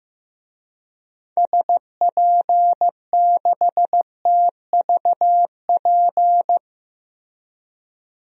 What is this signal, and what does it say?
Morse code, heard as a keyed tone, about 15 wpm, 700 Hz. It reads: SP6TVP
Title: DXCC-100 Call Signs - 6 Characters - Repeated 2x - 15wpm